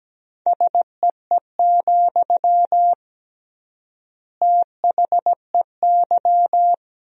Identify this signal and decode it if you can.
Morse code, heard as a keyed tone, about 17 wpm, 700 Hz. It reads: SEE, THEY
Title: 3x Word-Spacing - Sentences from Top 200 Words - 17wpm